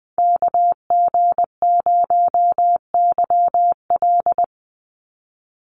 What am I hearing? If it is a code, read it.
Morse code, heard as a keyed tone, about 20 wpm, 700 Hz. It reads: KG0YL
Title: Call Signs Repeated 2x - 20wpm